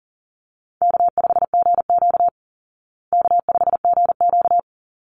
Morse code, heard as a keyed tone, about 40 wpm, 700 Hz. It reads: K5GQ K5GQ
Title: Call Signs Repeated 3x - 40wpm